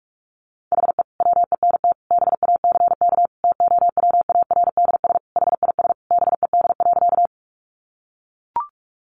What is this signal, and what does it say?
Morse code, heard as a keyed tone, about 44 wpm, 700 Hz. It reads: HE WENT BACK TOWARDS HIS BED.
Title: Sentences from Top 1100 Words 45wpm